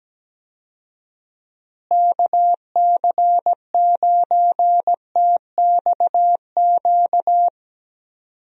Morse code, heard as a keyed tone, about 17 wpm, 700 Hz. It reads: KC9TXQ
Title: USA Call Signs - 6 Characters - Repeated 2x - 17wpm